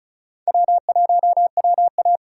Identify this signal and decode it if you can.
Morse code, heard as a keyed tone, about 34 wpm, 700 Hz. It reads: W1WA
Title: USA Call Signs - 4 Characters - 3x Character Spacing - 35wpm